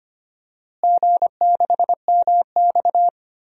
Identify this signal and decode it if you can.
Morse code, heard as a keyed tone, about 25 wpm, 700 Hz. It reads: G6MX
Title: DXCC-100 Call Signs - 4 Characters - Repeated 3x - 25wpm